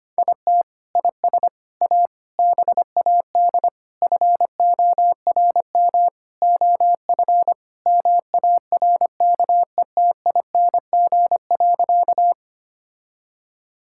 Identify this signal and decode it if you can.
Morse code, heard as a keyed tone, about 25 wpm, 700 Hz. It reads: IT IS A BAD FORM OF MARKETING.
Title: Sentences from Top 1000 Words - Review Part 5 - 25wpm